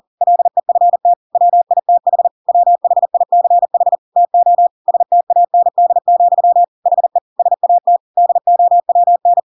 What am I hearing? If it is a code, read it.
Morse code, heard as a keyed tone, about 40 wpm, 700 Hz. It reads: LEFT WITH WHICH TO STAND, HE SAT DOWN
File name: Pinocchio / Pinocchio ch6 40 40 ik1hge